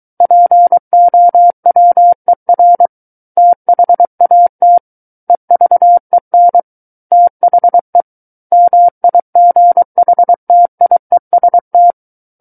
Morse code, heard as a keyed tone, about 23 wpm, 700 Hz. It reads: POWER THAT EVEN THE MIGHTIEST
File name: Book One / 14-Chapter 14 - 23 WPM